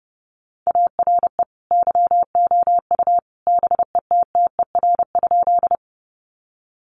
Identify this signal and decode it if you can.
Morse code, heard as a keyed tone, about 30 wpm, 700 Hz. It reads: ARE YOU BETTER?
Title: Sentences from Top 400 Words 30wpm